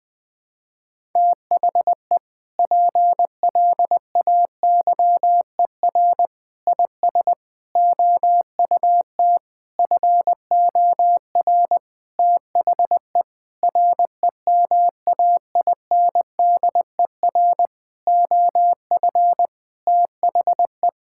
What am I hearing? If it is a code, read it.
Morse code, heard as a keyed tone, about 20 wpm, 700 Hz. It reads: THE PLAYER IS OUT FOR THE REMAINDER OF THE
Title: Sentences with 6-Word Phrases - 20wpm